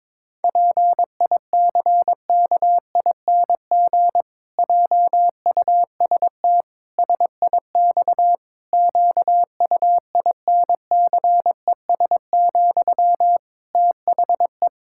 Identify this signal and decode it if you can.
Morse code, heard as a keyed tone, about 22 wpm, 700 Hz. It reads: PICKING JUST SIX QUINCES, THE
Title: Pangram Challenge v2 - 22wpm